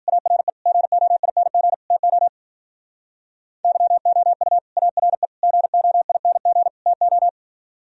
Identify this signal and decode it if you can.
Morse code, heard as a keyed tone, about 50 wpm, 700 Hz. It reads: ARE GOING TO YOU ARE GOING TO
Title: Speed-Racing - Sets of 4 Words 40wpm